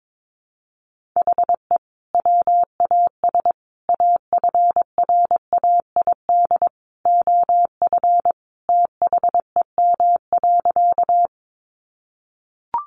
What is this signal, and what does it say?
Morse code, heard as a keyed tone, about 22 wpm, 700 Hz. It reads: HE WAS AFRAID OF THEM.
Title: Sentences from Top 1500 Words 22wpm